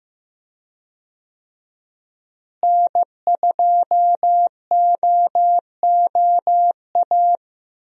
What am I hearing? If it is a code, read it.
Morse code, heard as a keyed tone, about 16 wpm, 700 Hz. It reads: N2OOA
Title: USA Call Signs - 5 Characters - Repeated 2x - 15wpm